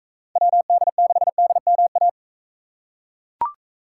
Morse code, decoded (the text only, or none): WD6DMA